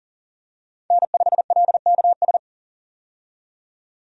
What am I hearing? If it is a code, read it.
Morse code, heard as a keyed tone, about 40 wpm, 700 Hz. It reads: N5LKS
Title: USA Call Signs - 5 Characters - 40wpm